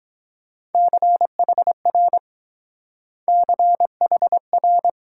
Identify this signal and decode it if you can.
Morse code, heard as a keyed tone, about 21 wpm, 700 Hz. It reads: CHR CHR